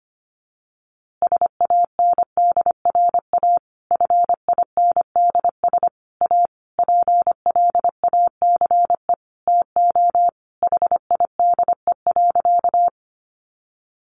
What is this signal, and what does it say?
Morse code, heard as a keyed tone, about 25 wpm, 700 Hz. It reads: SANDRA FINDS A PLACE TO HIDE.
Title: Easy Phrases 25wpm